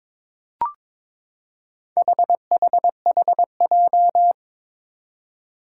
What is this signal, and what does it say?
Morse code, heard as a keyed tone, about 22 wpm, 700 Hz. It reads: HHHJ
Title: Four Challenging Letter-Number Combinations 22wpm